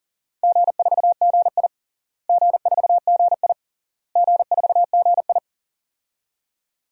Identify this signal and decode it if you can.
Morse code, heard as a keyed tone, about 40 wpm, 700 Hz. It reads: G4GI G4GI G4GI
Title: DXCC-100 Call Signs - 4 Characters - Repeated 3x - 40wpm